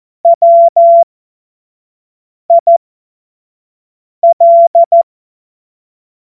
Morse code, heard as a keyed tone, about 14 wpm, spaced as for 5 wpm, 680 Hz. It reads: WIL
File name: Morsekurs / Witze / Witz 20. Dezember 2022 25 F70